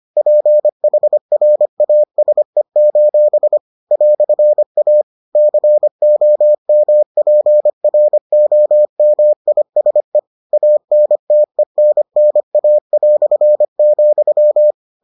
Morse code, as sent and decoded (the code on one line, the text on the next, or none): .--. .... .-. .- ... ---... / .-..-. .- / -.-. --- -- .--. .-. --- -- .. ... / .- -. - . -. -. .- .-..-. --..--
PHRASE: "A COMPROMISE ANTENNA",